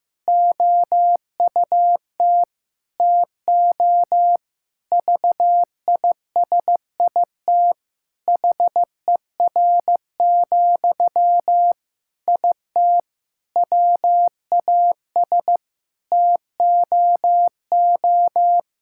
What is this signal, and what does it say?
Morse code, heard as a keyed tone, about 15 wpm, 700 Hz. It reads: OUT TO VISIT HER, IT WAS TOO
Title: Sentences from Top 1000 Words - Review Part 6 - 15wpm